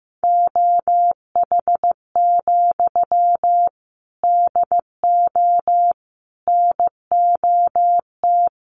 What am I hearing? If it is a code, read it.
Morse code, heard as a keyed tone, about 15 wpm, 700 Hz. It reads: OH, DO NOT
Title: Sentences from Top 1300 Words 15wpm